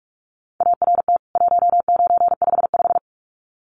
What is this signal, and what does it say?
Morse code, heard as a keyed tone, about 44 wpm, 700 Hz. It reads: ART 1955
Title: CWT Exchange 45wpm